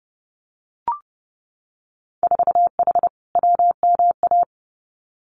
Morse code, heard as a keyed tone, about 30 wpm, 700 Hz. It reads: 4H WMA
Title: Winter Field Day Exchanges Encore 30wpm